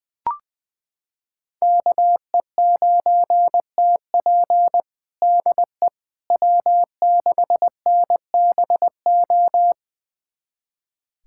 Morse code, heard as a keyed tone, about 20 wpm, 700 Hz. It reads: KE9TP DE W6NBO
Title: USA Callsign DE Callsign - 5 Characters - 20wpm